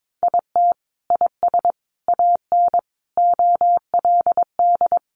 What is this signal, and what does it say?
Morse code, heard as a keyed tone, about 22 wpm, 700 Hz. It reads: IT IS AN OLD
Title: Sentences from Top 1500 Words 22wpm